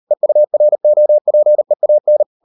Morse code, heard as a keyed tone, about 38 wpm, 600 Hz. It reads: EUROPEAN